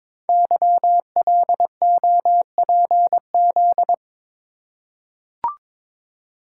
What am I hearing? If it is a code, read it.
Morse code, heard as a keyed tone, about 23 wpm, 700 Hz. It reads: YLOPZ
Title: Code Groups - 5 Letters - 22wpm